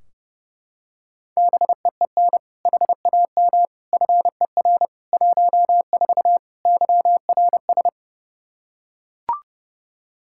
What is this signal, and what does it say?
Morse code, heard as a keyed tone, about 32 wpm, 700 Hz. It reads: BEEN HAM FER 14 YRS
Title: Practice QSO - Exchange 3 - More Info - 30wpm